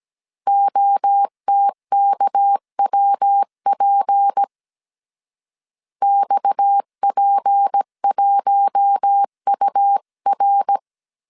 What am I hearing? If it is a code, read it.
Morse code, heard as a keyed tone, about 17 wpm, 800 Hz. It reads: OTKWP XP1UR
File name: CW / DET / 1 6 1500C S A 16wpm 2635